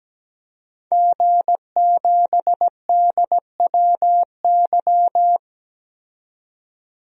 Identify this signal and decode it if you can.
Morse code, heard as a keyed tone, about 17 wpm, 700 Hz. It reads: G7DWY
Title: DXCC-100 Call Signs - 5 Characters - 17wpm